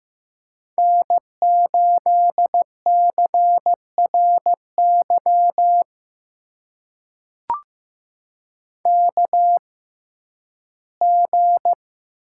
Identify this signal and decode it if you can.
Morse code, heard as a keyed tone, about 15 wpm, spaced as for 11 wpm, 700 Hz. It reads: N8CRY KG
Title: USA Call Signs - 6x Character Spacing - 15wpm